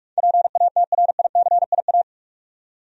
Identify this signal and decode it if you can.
Morse code, heard as a keyed tone, about 46 wpm, 700 Hz. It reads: PATRICIA
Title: USA Common Names - 45wpm